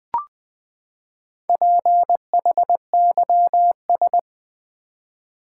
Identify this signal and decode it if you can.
Morse code, heard as a keyed tone, about 20 wpm, 700 Hz. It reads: PHYS
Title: Build A Word - 4 Letters At A Time - 8-Letter Words - 20wpm